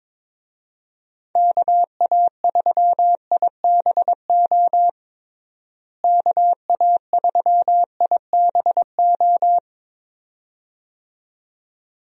Morse code, heard as a keyed tone, about 22 wpm, 700 Hz. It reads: KA3IBO KA3IBO
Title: USA Call Signs - 6 Characters - Repeated 2x - 22wpm